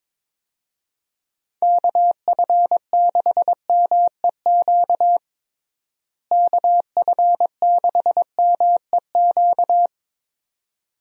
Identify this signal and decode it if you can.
Morse code, heard as a keyed tone, about 22 wpm, 700 Hz. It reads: KF6MEQ KF6MEQ
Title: USA Call Signs - 6 Characters - Repeated 2x - 22wpm